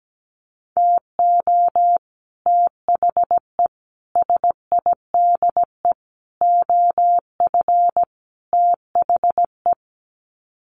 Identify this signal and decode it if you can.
Morse code, heard as a keyed tone, about 17 wpm, 700 Hz. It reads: TO THE SIDE OF THE
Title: Sets of 5 Words 17wpm